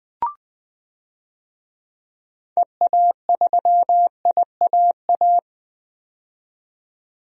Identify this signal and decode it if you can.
Morse code, heard as a keyed tone, about 20 wpm, 700 Hz. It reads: EA3IAA
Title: DXCC-100 Call Signs - 6 Characters - 20wpm